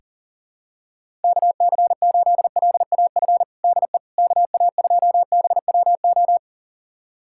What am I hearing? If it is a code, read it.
Morse code, heard as a keyed tone, about 37 wpm, 700 Hz. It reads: KC8LAF DE KA2BWO
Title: USA Callsign DE Callsign - 6 Characters - 40wpm